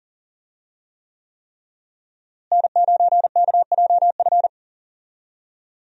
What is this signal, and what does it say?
Morse code, heard as a keyed tone, about 40 wpm, 700 Hz. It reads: N9KJF
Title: USA Call Signs - 5 Characters - Repeated 3x - 40wpm